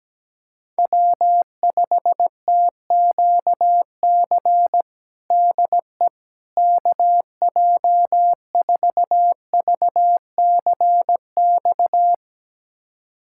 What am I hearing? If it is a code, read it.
Morse code, heard as a keyed tone, about 17 wpm, 700 Hz. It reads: W5TQC DE KJ4VCX
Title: USA Callsign DE Callsign 17wpm